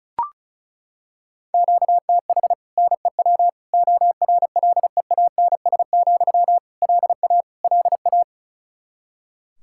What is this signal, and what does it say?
Morse code, heard as a keyed tone, about 36 wpm, 700 Hz. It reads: QTH NEW ORLEANS, LA LA
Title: Practice QSO - Exchange 1 - Beginning - 35wpm